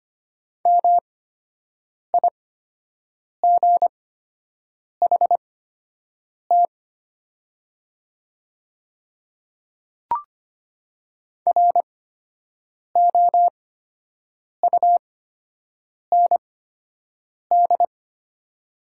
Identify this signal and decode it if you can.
Morse code, heard as a keyed tone, about 25 wpm, spaced as for 7 wpm, 700 Hz. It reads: MIGHT ROUND
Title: 8x Character Spacing - Top 300 Words - 25wpm